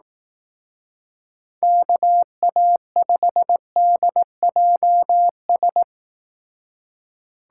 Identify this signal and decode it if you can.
Morse code, heard as a keyed tone, about 18 wpm, 700 Hz. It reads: KA5DJS
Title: Speed Racing - USA 2x3 Call Signs 15wpm